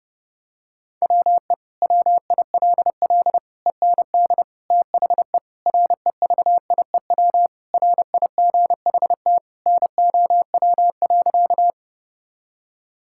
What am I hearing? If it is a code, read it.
Morse code, heard as a keyed tone, about 30 wpm, 700 Hz. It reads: WE WILL END THE REVIEW RIGHT NOW.
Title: Sentences from Top 1000 Words - Review Part 6 - 30wpm